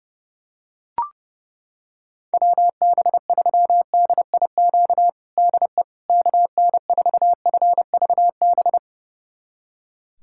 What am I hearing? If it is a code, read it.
Morse code, heard as a keyed tone, about 32 wpm, 700 Hz. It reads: WB3DIQ DE KN4FVB
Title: USA Callsign DE Callsign - 6 Characters - 30wpm